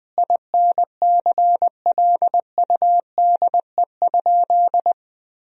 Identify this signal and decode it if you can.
Morse code, heard as a keyed tone, about 21 wpm, 700 Hz. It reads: INCLUDE?
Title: Sentences from Top 1000 Words - Review Part 3 - 20wpm